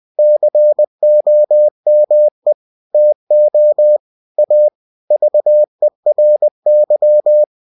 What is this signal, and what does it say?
Morse code, heard as a keyed tone, about 20 wpm, 600 Hz. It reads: COME TO A VERY
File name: cw-books / winnie-the-pooh / winnie-20wpm-page-0009